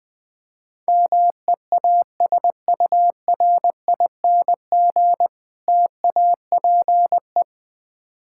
Morse code, heard as a keyed tone, about 21 wpm, 700 Hz. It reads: MEASURING TAPE
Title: Difficult Words with Hints - 20wpm